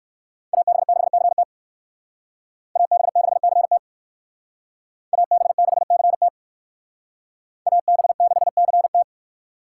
Morse code, heard as a keyed tone, about 57 wpm, 700 Hz. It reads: AB6CT AB6CT AB6CT AB6CT